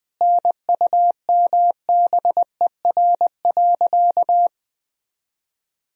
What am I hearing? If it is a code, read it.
Morse code, heard as a keyed tone, about 20 wpm, 700 Hz. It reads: NUMBER.
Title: Sentences from Top 1000 Words - Review Part 1 - 20wpm